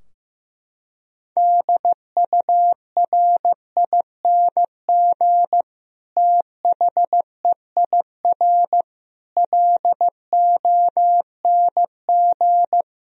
Morse code, heard as a keyed tone, about 16 wpm, 700 Hz. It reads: DURING THEIR LONG
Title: Sentences from Top 1000 Words - Review Part 2 - 15wpm